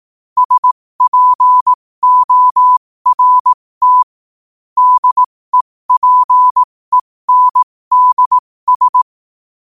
Morse code, heard as a keyed tone, about 18 wpm, spaced as for 15 wpm, 1000 Hz. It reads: SPORT DEPENDS
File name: qotdfiles / 15wpm